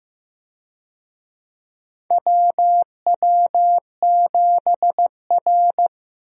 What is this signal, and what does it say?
Morse code, heard as a keyed tone, about 16 wpm, 700 Hz. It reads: WW7R
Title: USA Call Signs - 4 Characters - 15wpm